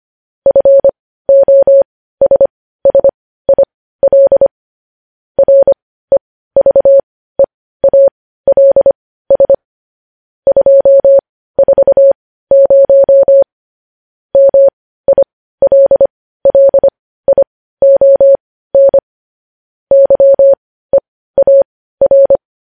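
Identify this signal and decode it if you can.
Morse code, heard as a keyed tone, about 25 wpm, spaced as for 15 wpm, 550 Hz. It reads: FOSSIL REVEALS 240 MILLION YEAR
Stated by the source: Morse practice of news headlines at 15 wpm.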